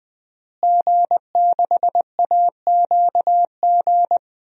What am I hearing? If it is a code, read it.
Morse code, heard as a keyed tone, about 20 wpm, 700 Hz. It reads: G6AQG
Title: DXCC-100 Call Signs - 5 Characters - Repeated 3x - 20wpm